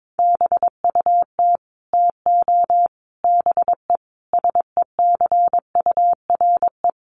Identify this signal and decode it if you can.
Morse code, heard as a keyed tone, about 22 wpm, 700 Hz. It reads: BUT TO BE SECURE
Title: Sentences from Top 1200 Words 22wpm